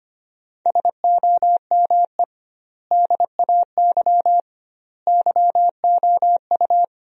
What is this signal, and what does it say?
Morse code, heard as a keyed tone, about 25 wpm, 700 Hz. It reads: SOME DAY YOU
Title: 2x Word-Spacing - Sentences from Top 200 Words - 25wpm